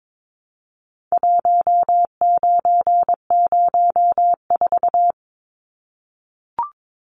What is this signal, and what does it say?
Morse code, heard as a keyed tone, about 22 wpm, 700 Hz. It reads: 1904
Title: Numbers - 4-Digit 22wpm